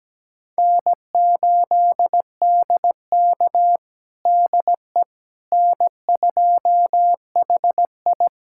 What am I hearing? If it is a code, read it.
Morse code, heard as a keyed tone, about 17 wpm, 700 Hz. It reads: N8DK DE N2HI
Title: USA Callsign DE Callsign - 4 Characters - 17wpm